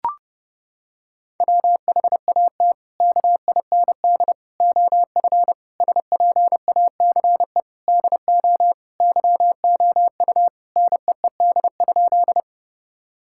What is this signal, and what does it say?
Morse code, heard as a keyed tone, about 30 wpm, 700 Hz. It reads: WHAT KIND OF SPACE DO YOU NEED?
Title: Sentences from Top 1000 Words - Review Part 5 - 30wpm